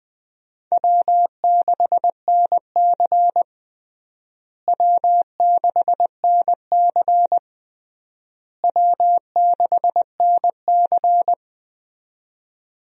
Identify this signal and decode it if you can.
Morse code, heard as a keyed tone, about 20 wpm, 700 Hz. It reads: W6NC W6NC W6NC
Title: USA Call Signs - 4 Characters - Repeated 3x - 20wpm